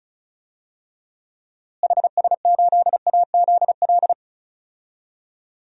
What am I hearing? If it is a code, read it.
Morse code, heard as a keyed tone, about 36 wpm, 700 Hz. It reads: HS8AZL
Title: DXCC-100 Call Signs - 6 Characters - Repeated 2x - 35wpm